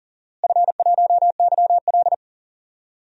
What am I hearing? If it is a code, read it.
Morse code, heard as a keyed tone, about 40 wpm, 700 Hz. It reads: F1YL
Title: DXCC-100 Call Signs - 4 Characters - Repeated 3x - 40wpm